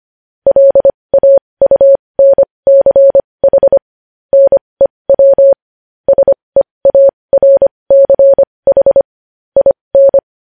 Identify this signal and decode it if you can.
Morse code, heard as a keyed tone, about 25 wpm, spaced as for 20 wpm, 550 Hz. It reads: LAUNCH NEW SEARCH IN
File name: podcast / News headlines-1x-25x20